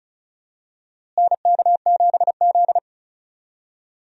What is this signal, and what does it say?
Morse code, heard as a keyed tone, about 34 wpm, 700 Hz. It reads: NK7Z
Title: USA Call Signs - 4 Characters - Repeated 3x - 35wpm